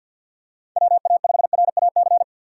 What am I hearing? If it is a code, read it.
Morse code, heard as a keyed tone, about 48 wpm, 700 Hz. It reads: WA5RAC
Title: USA Call Signs - 6 Characters - 4x Character Spacing - 50wpm